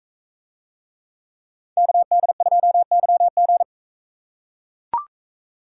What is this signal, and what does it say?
Morse code, heard as a keyed tone, about 40 wpm, 700 Hz. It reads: KD2YG